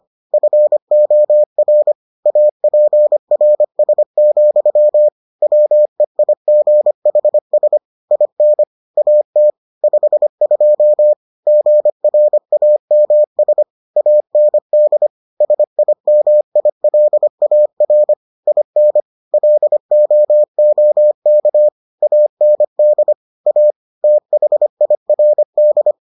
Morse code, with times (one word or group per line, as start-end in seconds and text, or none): 0.3-1.9 FOR
2.3-5.1 APRS,
5.4-7.8 WEIGHS
8.1-8.6 IN
9.0-9.5 AT
9.8-11.1 52
11.5-13.6 GRAMS
14.0-15.1 AND
15.4-18.1 SIMILAR
18.5-19.0 IN
19.3-21.7 LOOK
22.0-23.1 AND
23.5-23.7 A
24.0-25.9 THIRD